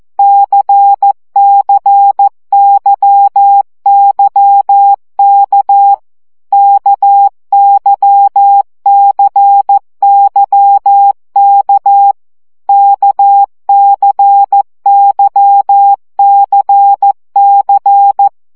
Morse code, heard as a keyed tone, about 14 wpm, 800 Hz. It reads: CCYYK KYCYK KCYCC
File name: CW gyakorlas k c y 12 baud